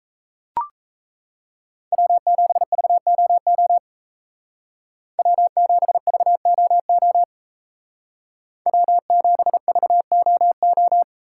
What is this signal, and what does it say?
Morse code, heard as a keyed tone, about 34 wpm, 700 Hz. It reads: W7VOO W7VOO W7VOO
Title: Speed Racing - USA 1x3 Call Signs 28wpm